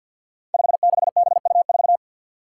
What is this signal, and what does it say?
Morse code, heard as a keyed tone, about 46 wpm, 700 Hz. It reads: 56BU4
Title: Five Challenging Letter-Number Combinations 50wpm